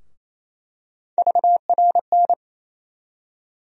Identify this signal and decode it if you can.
Morse code, heard as a keyed tone, about 28 wpm, 700 Hz. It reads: VRN
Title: USA Call Sign Postfixes - 3 Letters 28wpm